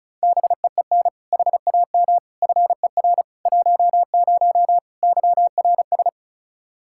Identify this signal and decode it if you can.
Morse code, heard as a keyed tone, about 34 wpm, 700 Hz. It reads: BEEN HAM FER 10 YRS
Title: Practice QSO - Exchange 3 - More Info - 35wpm